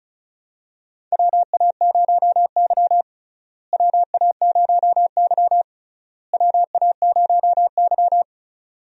Morse code, heard as a keyed tone, about 34 wpm, 700 Hz. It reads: WA0Y WA0Y WA0Y
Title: USA Call Signs - 4 Characters - Repeated 3x - 35wpm